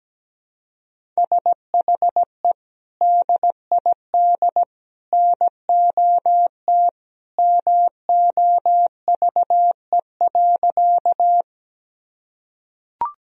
Morse code, heard as a keyed tone, about 17 wpm, 700 Hz. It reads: SHE DID NOT MOVE.
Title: Sentences from Top 1000 Words - Review Part 1 - 17wpm